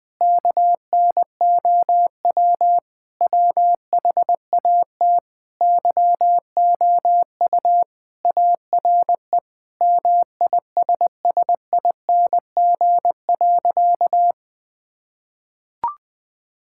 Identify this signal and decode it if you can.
Morse code, heard as a keyed tone, about 20 wpm, 700 Hz. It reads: KNOW WHAT YOU ARE MISSING.
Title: Sentences with 5-Word Phrases - Spoken Beginning - 20wpm